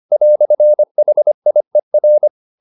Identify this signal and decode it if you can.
Morse code, heard as a keyed tone, about 25 wpm, 600 Hz. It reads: "HIER